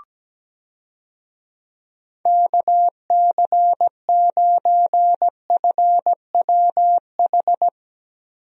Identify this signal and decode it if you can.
Morse code, heard as a keyed tone, about 17 wpm, 700 Hz. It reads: KC9FWH
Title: USA Call Signs - 6 Characters - Repeated 2x - 17wpm